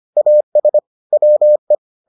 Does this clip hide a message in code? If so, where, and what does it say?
Yes: Morse code from 0.2 to 1.7 s, heard as a keyed tone, about 25 wpm, 600 Hz. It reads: AS WE